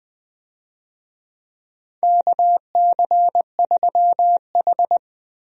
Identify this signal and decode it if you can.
Morse code, heard as a keyed tone, about 20 wpm, 700 Hz. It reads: KC3H
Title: USA Call Signs - 4 Characters - Repeated 2x - 20wpm